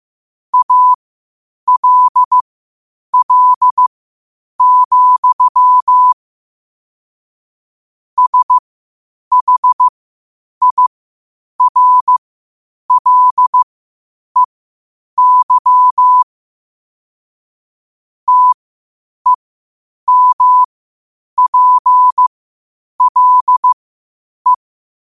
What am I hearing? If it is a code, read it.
Morse code, heard as a keyed tone, about 15 wpm, spaced as for 8 wpm, 1000 Hz. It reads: ALL, SHIRLEY TEMPLE